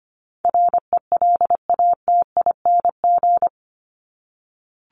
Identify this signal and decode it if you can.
Morse code, heard as a keyed tone, about 25 wpm, 700 Hz. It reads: RELATING
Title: ING Suffix - Top 100 Words with Base Words 25wpm